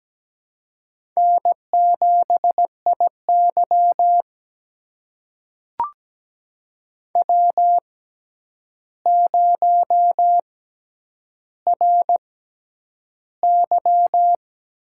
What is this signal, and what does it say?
Morse code, heard as a keyed tone, about 17 wpm, spaced as for 9 wpm, 700 Hz. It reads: N7IY W0RY
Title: USA Call Signs - 4 Characters - 6x Character Spacing - 17wpm